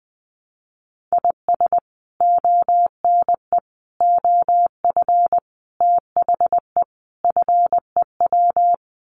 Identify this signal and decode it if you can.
Morse code, heard as a keyed tone, about 20 wpm, 700 Hz. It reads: IS ONE OF THE FEW
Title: Sets of 5 Words 20wpm